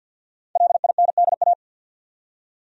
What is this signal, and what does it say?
Morse code, heard as a keyed tone, about 53 wpm, 700 Hz. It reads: LINDA
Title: USA Common Names - Max 5 Letters - 50wpm